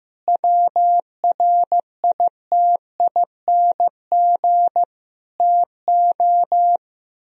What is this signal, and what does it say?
Morse code, heard as a keyed tone, about 15 wpm, 700 Hz. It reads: WRITING TO
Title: Sentences from Top 1000 Words - Review Part 4 - 15wpm